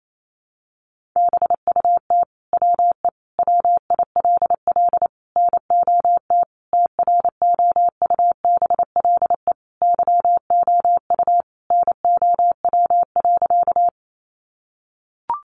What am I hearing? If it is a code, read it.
Morse code, heard as a keyed tone, about 28 wpm, 700 Hz. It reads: BUT WE WILL NOT TROUBLE YOU NOW.
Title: Sentences from Top 1200 Words 28wpm